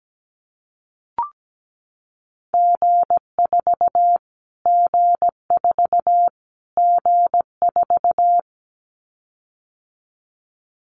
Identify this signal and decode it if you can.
Morse code, heard as a keyed tone, about 17 wpm, 700 Hz. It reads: G4 G4 G4
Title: Callsign Prefixes for Top 25 DXCC - Repeated 3x - 17wpm